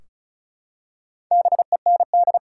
Morse code, heard as a keyed tone, about 36 wpm, 700 Hz. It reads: BEND
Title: Four Letter Words - 35wpm